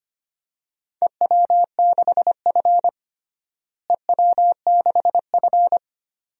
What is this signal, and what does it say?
Morse code, heard as a keyed tone, about 25 wpm, 700 Hz. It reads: EW6F EW6F
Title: DXCC-100 Call Signs - 4 Characters - Repeated 2x - 25wpm